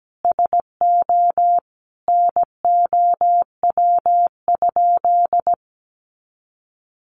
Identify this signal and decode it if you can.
Morse code, heard as a keyed tone, about 17 wpm, 700 Hz. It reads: SO NOW?
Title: Sentences from Top 200 Words 17wpm